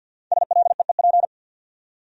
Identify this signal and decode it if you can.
Morse code, heard as a keyed tone, about 59 wpm, 700 Hz. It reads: SLEEP
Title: Speed-Racing Top 500 Words 50wpm